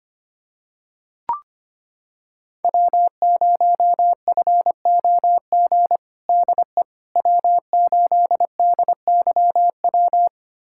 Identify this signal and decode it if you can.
Morse code, heard as a keyed tone, about 25 wpm, 700 Hz. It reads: W0FOG DE W8DYW
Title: USA Callsign DE Callsign - 5 Characters - 25wpm